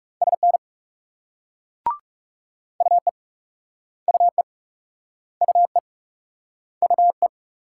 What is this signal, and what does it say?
Morse code, heard as a keyed tone, about 37 wpm, 700 Hz. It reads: SN UE UE UE UE
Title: Speed-Racing Multi-letter Phonemes - 30wpm